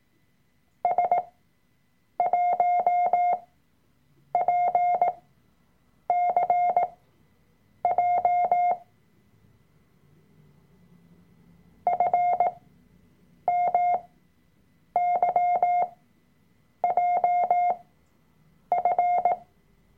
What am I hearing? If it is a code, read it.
Morse code, heard as a keyed tone, about 18 wpm, spaced as for 7 wpm, 710 Hz. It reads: S1PCJ FMYJF
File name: Workshop Dateien / CW-Online-Kurs / Lektion 003